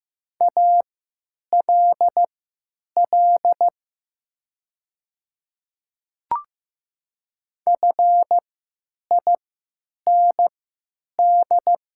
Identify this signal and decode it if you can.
Morse code, heard as a keyed tone, about 15 wpm, spaced as for 9 wpm, 700 Hz. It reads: ALL FIND
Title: 3x Character Spacing - Top 100 Words - 15wpm